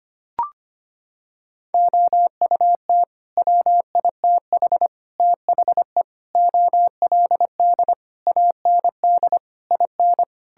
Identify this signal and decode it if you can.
Morse code, heard as a keyed tone, about 25 wpm, 700 Hz. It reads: OUT WITH THE OLD AND IN
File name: Sets of 9 Words - 25wpm